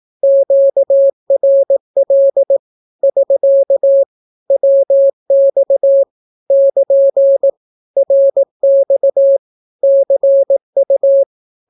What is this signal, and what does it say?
Morse code, heard as a keyed tone, about 18 wpm, 550 Hz. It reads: QRL <SK> WX <KN> RX CU
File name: naukapodstaw / skroty / skroty